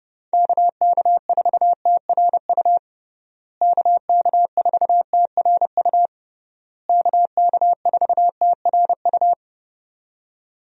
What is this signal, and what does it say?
Morse code, heard as a keyed tone, about 30 wpm, 700 Hz. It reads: KK4TRU KK4TRU KK4TRU
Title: USA Call Signs - 6 Characters - Repeated 3x - 30wpm